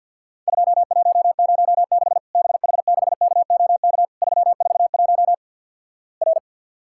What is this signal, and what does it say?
Morse code, heard as a keyed tone, about 50 wpm, 700 Hz. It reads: 2106 BH6KOK 341 R
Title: Worked All Europe DX Contest - QTC Exchange - 50wpm